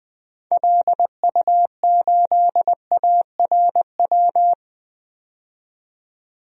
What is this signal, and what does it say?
Morse code, heard as a keyed tone, about 21 wpm, 700 Hz. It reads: LU8ARW
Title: DXCC-100 Call Signs - 6 Characters - 20wpm